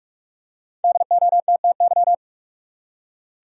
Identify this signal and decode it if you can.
Morse code, heard as a keyed tone, about 46 wpm, 700 Hz. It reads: DOTTY
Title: USA Common Names - 45wpm